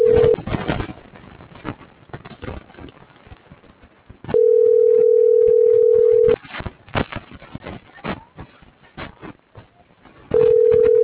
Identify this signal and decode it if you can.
Ringback tone (North America), 440 + 480 Hz.